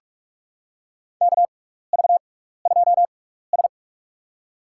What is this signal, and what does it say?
Morse code, heard as a keyed tone, about 44 wpm, spaced as for 15 wpm, 700 Hz. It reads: KV2S
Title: USA Call Signs - 4 Characters - 6x Character Spacing - 45wpm